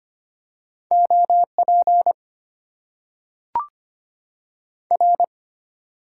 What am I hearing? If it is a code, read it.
Morse code, heard as a keyed tone, about 25 wpm, 700 Hz. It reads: OP R
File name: Top 90 QSO Elements - 8x Character Spacing - 25wpm